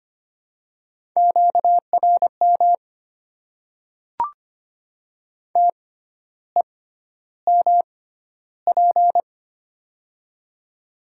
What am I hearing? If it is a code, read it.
Morse code, heard as a keyed tone, about 25 wpm, spaced as for 12 wpm, 700 Hz. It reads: QRM TEMP
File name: Top 90 QSO Elements - 6x Character Spacing - 25wpm